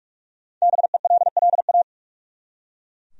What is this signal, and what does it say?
Morse code, heard as a keyed tone, about 43 wpm, 700 Hz. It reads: BELLA
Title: USA Common Names - Max 5 Letters - 45wpm